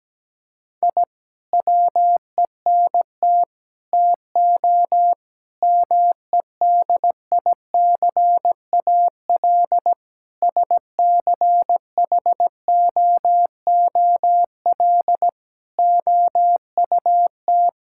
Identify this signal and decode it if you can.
Morse code, heard as a keyed tone, about 17 wpm, 700 Hz. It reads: I WENT TO MEDICAL SCHOOL OUT
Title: Sentences from Top 1000 Words - Review Part 5 - 17wpm